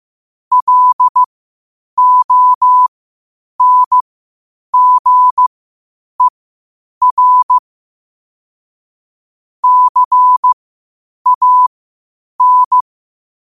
Morse code, heard as a keyed tone, about 15 wpm, spaced as for 8 wpm, 1000 Hz. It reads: LONGER CAN
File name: qotdfiles / 8wpm